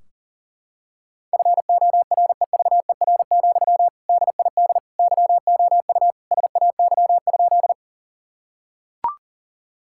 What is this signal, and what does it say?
Morse code, heard as a keyed tone, about 40 wpm, 700 Hz. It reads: FOREVER, DID YOU SAY?
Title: Sentences from Top 1600 Words 40wpm